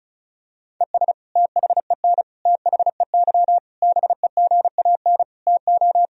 Morse code, heard as a keyed tone, about 34 wpm, 700 Hz. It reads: ES THEN THEY BEGAN TO
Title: Sentences from Top 300 Words and Abbreviations 35wpm